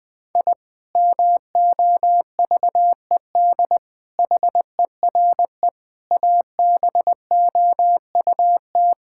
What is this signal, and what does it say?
Morse code, heard as a keyed tone, about 21 wpm, 700 Hz. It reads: I MOVED HERE ABOUT
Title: Sentences with 7-Word Phrases - Spoken Beginning - 20wpm